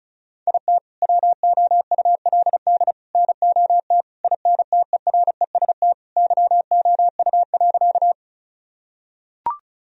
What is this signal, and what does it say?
Morse code, heard as a keyed tone, about 34 wpm, 700 Hz. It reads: IT WOULD NOT INTEREST YOU.
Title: Sentences from Top 1000 Words - Review Part 2 - 35wpm